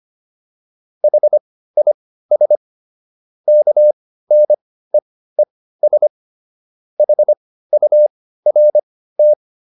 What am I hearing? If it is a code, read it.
Morse code, heard as a keyed tone, about 25 wpm, spaced as for 15 wpm, 600 Hz. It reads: HIS KNEES HURT